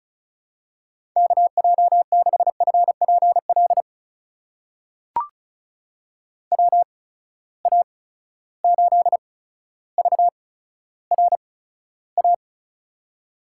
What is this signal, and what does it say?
Morse code, heard as a keyed tone, about 36 wpm, spaced as for 15 wpm, 700 Hz. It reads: KJ6FPL WA8VRA
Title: USA Call Signs - 6 Characters - 8x Character Spacing - 35wpm